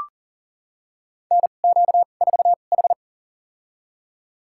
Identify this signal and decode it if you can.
Morse code, heard as a keyed tone, about 40 wpm, spaced as for 29 wpm, 700 Hz. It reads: NQ4H
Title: USA Call Signs - 4 Characters - 2x Character Spacing - 40wpm